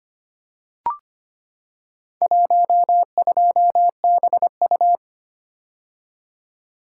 Morse code, heard as a keyed tone, about 25 wpm, 700 Hz. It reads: 12BU
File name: practice / 25wpm / Four Letter-Number Combinations - 25wpm